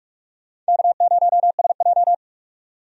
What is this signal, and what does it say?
Morse code, heard as a keyed tone, about 41 wpm, 700 Hz. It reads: K0SJ
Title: USA Call Signs - 4 Characters - 45wpm